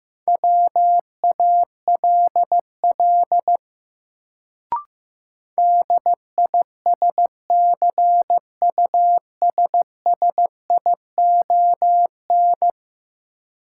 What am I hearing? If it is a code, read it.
Morse code, heard as a keyed tone, about 16 wpm, 700 Hz. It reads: WALL DISCUSSION
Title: Sentences from Top 1100 Words - New Word Drill Alt - 15wpm